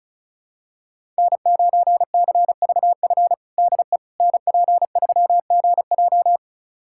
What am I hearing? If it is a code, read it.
Morse code, heard as a keyed tone, about 34 wpm, 700 Hz. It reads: N9CVF DE NP3GJ
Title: USA Callsign DE Callsign - 5 Characters - 35wpm